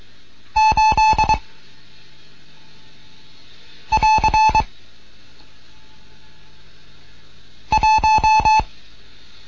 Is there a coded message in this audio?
Yes: Morse code from 0.6 to 8.6 s, heard as a keyed tone, about 22 wpm, 870 Hz. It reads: 8 <AR> 1